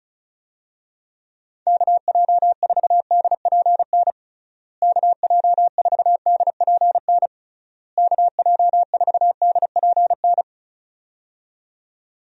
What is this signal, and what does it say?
Morse code, heard as a keyed tone, about 34 wpm, 700 Hz. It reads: KJ4DPN KJ4DPN KJ4DPN
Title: USA Call Signs - 6 Characters - Repeated 3x - 35wpm